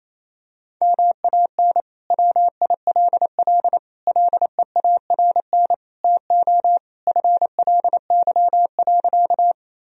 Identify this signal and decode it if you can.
Morse code, heard as a keyed tone, about 28 wpm, 700 Hz. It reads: MAN WILL LEARN TO FLY.
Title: Sentences from Top 1000 Words - Review Part 2 - 28wpm